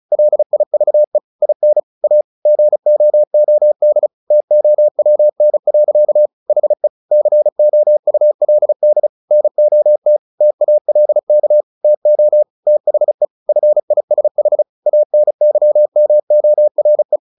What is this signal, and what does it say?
Morse code, heard as a keyed tone, about 34 wpm, 600 Hz. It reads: LIVE IN A GOOD TOWN. HE COULD NOT TALK TO THE FISH ANYMORE
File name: the old man and the sea 2 35wpm